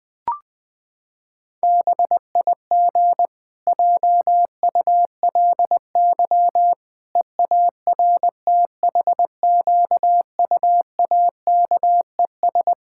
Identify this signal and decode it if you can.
Morse code, heard as a keyed tone, about 20 wpm, 700 Hz. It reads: BIG JULY EARTHQUAKES
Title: Pangram Challenge v2 - 20wpm